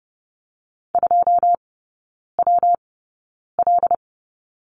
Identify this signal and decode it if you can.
Morse code, heard as a keyed tone, about 30 wpm, spaced as for 9 wpm, 700 Hz. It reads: 2WL
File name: Lesson 19e - Letter w - Three Characters 30wpm